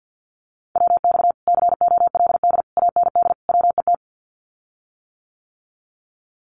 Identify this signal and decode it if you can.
Morse code, heard as a keyed tone, about 50 wpm, 700 Hz. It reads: WX COLD AND WET
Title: Common QSO Phrases Encore 50wpm